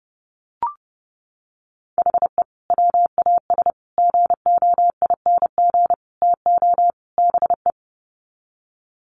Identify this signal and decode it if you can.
Morse code, heard as a keyed tone, about 30 wpm, 700 Hz. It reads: HE WAS GOING TO BE
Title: Sets of 5 Words 30wpm